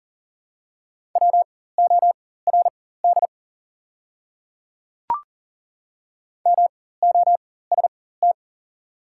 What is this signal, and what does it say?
Morse code, heard as a keyed tone, about 40 wpm, spaced as for 19 wpm, 700 Hz. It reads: WORD MOST
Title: 4x Character Spacing - Top 100 Words - 40wpm